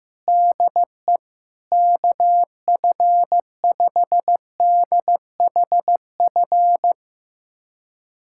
Morse code, heard as a keyed tone, about 15 wpm, 700 Hz. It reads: DE KF5DHF
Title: USA Callsign DE Callsign - 6 Characters - 15wpm